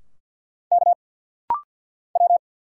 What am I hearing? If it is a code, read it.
Morse code, heard as a keyed tone, about 50 wpm, 690 Hz. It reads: K W
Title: Single Letters - Rapid-Fire 50wpm